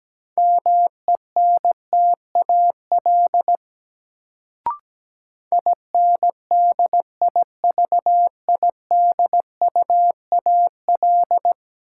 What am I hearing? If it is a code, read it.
Morse code, heard as a keyed tone, about 17 wpm, 700 Hz. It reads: MENTAL INDIVIDUAL
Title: Sentences from Top 1000 Words - Review Part 4 - New Word Drill - 17wpm